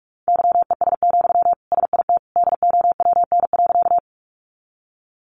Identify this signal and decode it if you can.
Morse code, heard as a keyed tone, about 44 wpm, 700 Hz. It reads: YES, SIT DOWN.
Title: Sentences from Top 500 Words 45wpm